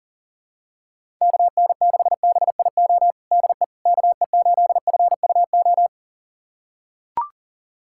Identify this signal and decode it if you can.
Morse code, heard as a keyed tone, about 40 wpm, 700 Hz. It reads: KN6BIO DE KE8FUO
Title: USA Callsign DE Callsign - 6 Characters - 40wpm